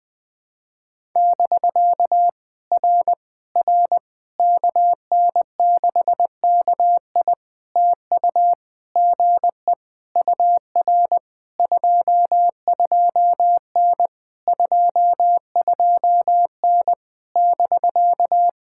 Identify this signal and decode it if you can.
Morse code, heard as a keyed tone, about 20 wpm, 700 Hz. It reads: <BK> R R KN6KI TU GE UR 22N 22N <BK>
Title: POTA QSO Practice - 20wpm